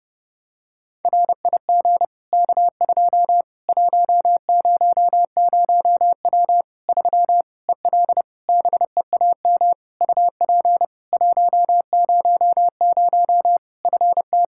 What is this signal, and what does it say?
Morse code, heard as a keyed tone, about 32 wpm, 700 Hz. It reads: RIG K2 100W 3 EL BEAM UP 100 FT
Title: K1USN SST QSO Practice 30wpm